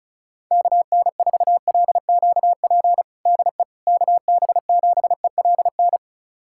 Morse code, heard as a keyed tone, about 36 wpm, 700 Hz. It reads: KN4LQP DE KB7ELN
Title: USA Callsign DE Callsign 35wpm